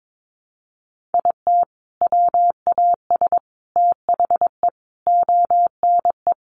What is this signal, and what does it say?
Morse code, heard as a keyed tone, about 22 wpm, 700 Hz. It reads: IT WAS THE ONE
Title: Sentences from Top 1400 Words 22wpm